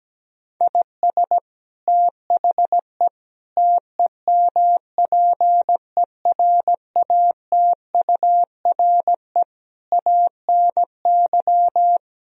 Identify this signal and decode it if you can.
Morse code, heard as a keyed tone, about 17 wpm, 700 Hz. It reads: IS THE TEMPERATURE ANY
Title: Sentences from Top 1000 Words - Review Part 4 - 17wpm